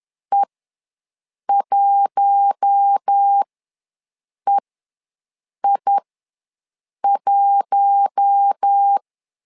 Morse code, heard as a keyed tone, about 10 wpm, spaced as for 6 wpm, 800 Hz. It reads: E1EI1
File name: LEZIONI / 4 Q1 ws 10wpm 2538